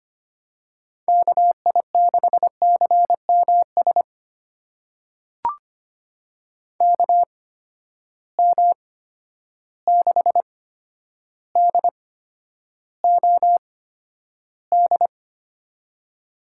Morse code, heard as a keyed tone, about 25 wpm, spaced as for 11 wpm, 700 Hz. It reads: KI6CMS KM6DOD
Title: USA Call Signs - 6 Characters - 8x Character Spacing - 25wpm